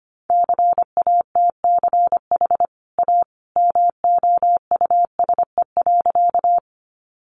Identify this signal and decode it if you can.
Morse code, heard as a keyed tone, about 25 wpm, 700 Hz. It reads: CATCH A MOUSE.
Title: Sentences from Top 1500 Words 25wpm